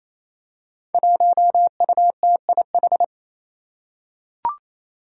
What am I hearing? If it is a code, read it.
Morse code, heard as a keyed tone, about 28 wpm, 700 Hz. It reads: 1UTIH
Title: Five Challenging Letter-Number Combinations 28wpm